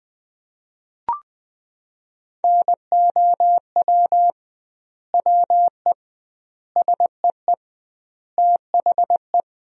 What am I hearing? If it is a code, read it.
Morse code, heard as a keyed tone, about 20 wpm, 700 Hz. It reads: NOW WE SEE THE
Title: 2x Word-Spacing - Sentences from Top 200 Words - 20wpm